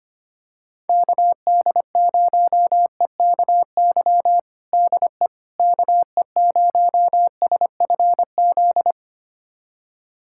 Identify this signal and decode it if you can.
Morse code, heard as a keyed tone, about 25 wpm, 700 Hz. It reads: KD0EKY DE KE0SFZ
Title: USA Callsign DE Callsign - 6 Characters - 25wpm